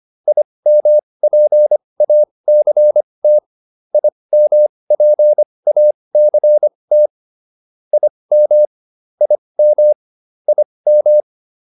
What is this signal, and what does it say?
Morse code, heard as a keyed tone, about 25 wpm, spaced as for 20 wpm, 600 Hz. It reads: IMPACT IMPACT IM IM IM